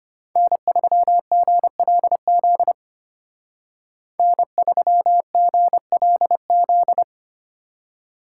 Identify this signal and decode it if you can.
Morse code, heard as a keyed tone, about 33 wpm, 700 Hz. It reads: N3GLZ N3GLZ